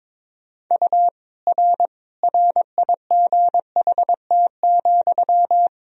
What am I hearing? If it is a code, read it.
Morse code, heard as a keyed tone, about 22 wpm, 700 Hz. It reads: U R RIGHT,
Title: Sentences from Top 400 Words and Abbreviations 22wpm